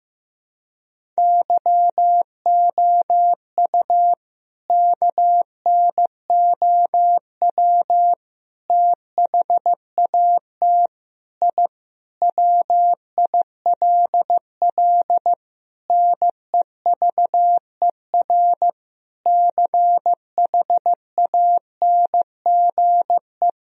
Morse code, heard as a keyed tone, about 15 wpm, 700 Hz. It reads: YOU KNOW THAT I WILL NEVER CHANGE
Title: Sentences from Top 1000 Words - Review Part 1 - 15wpm